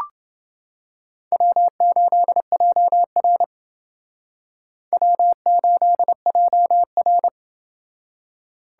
Morse code, heard as a keyed tone, about 29 wpm, 700 Hz. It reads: W8JR W8JR